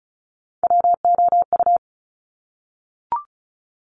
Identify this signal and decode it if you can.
Morse code, heard as a keyed tone, about 36 wpm, 700 Hz. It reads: WOU
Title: Top Three Letter Combinations in Words 35wpm